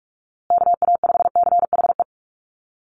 Morse code, heard as a keyed tone, about 44 wpm, 700 Hz. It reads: KA5CHE
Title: USA Call Signs Repeated 3x 45wpm